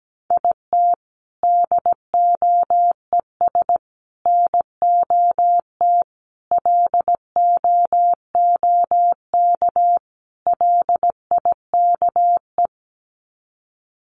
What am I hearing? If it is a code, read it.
Morse code, heard as a keyed tone, about 17 wpm, 700 Hz. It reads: IT DOES NOT LOOK LIKE
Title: Sets of 5 Words Encore2 17wpm